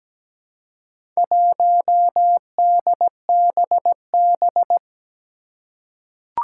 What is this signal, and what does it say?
Morse code, heard as a keyed tone, about 17 wpm, 700 Hz. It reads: 1DBB
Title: Four Challenging Letter-Number Combinations 17wpm